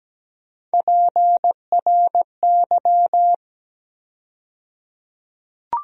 Morse code, heard as a keyed tone, about 17 wpm, 700 Hz. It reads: PRY
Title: USA Call Sign Postfixes - 3 Letters 17wpm